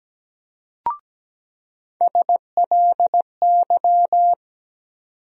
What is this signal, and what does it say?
Morse code, heard as a keyed tone, about 17 wpm, 700 Hz. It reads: SLY
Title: USA Call Sign Postfixes - 3 Letters 17wpm